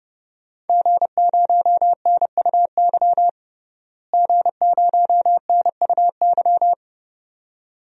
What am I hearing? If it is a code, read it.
Morse code, heard as a keyed tone, about 32 wpm, 700 Hz. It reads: G0NUY G0NUY
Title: DXCC-100 Call Signs - 5 Characters - Repeated 2x - 30wpm